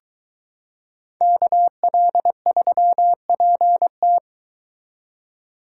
Morse code, heard as a keyed tone, about 22 wpm, 700 Hz. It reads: KL3PT